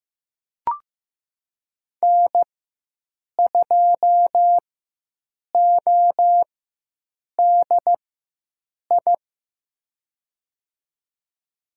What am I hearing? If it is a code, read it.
Morse code, heard as a keyed tone, about 16 wpm, spaced as for 7 wpm, 700 Hz. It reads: N2ODI
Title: USA Call Signs - 5 Characters - 4x Character Spacing - 15wpm